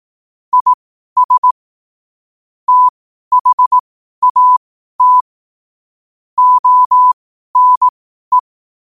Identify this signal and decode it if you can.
Morse code, heard as a keyed tone, about 18 wpm, spaced as for 12 wpm, 1000 Hz. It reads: IS THAT ONE
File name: qotdfiles / 12wpm